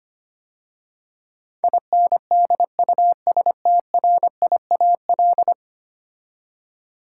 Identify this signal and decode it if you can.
Morse code, heard as a keyed tone, about 25 wpm, 700 Hz. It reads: INDUSTRIAL
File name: 10 Letter Words - First 3 Letters Given - 25wpm